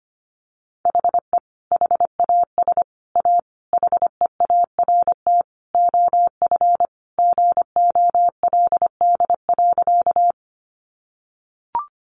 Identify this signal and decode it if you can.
Morse code, heard as a keyed tone, about 25 wpm, 700 Hz. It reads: HE HAS A HEART OF GOLD.
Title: Sentences from Top 1200 Words 25wpm